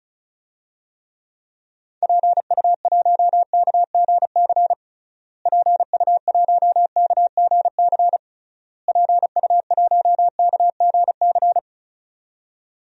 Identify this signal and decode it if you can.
Morse code, heard as a keyed tone, about 34 wpm, 700 Hz. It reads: PU1KGC PU1KGC PU1KGC
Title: DXCC-100 Call Signs - 6 Characters - Repeated 3x - 35wpm